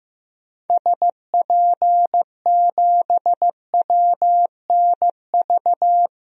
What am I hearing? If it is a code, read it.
Morse code, heard as a keyed tone, about 15 wpm, 700 Hz. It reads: SP7WNV
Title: DXCC-100 Call Signs - 6 Characters - 15wpm